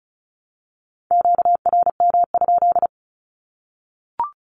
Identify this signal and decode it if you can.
Morse code, heard as a keyed tone, about 34 wpm, 700 Hz. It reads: QRM?
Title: Q-Codes 35wpm